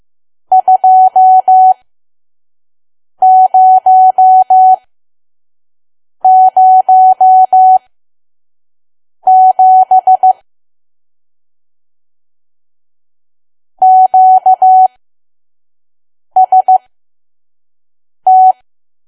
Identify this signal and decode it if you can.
Morse code, heard as a keyed tone, about 15 wpm, spaced as for 5 wpm, 750 Hz. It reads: 2007 QST
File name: Archive / 5 WPM files / PM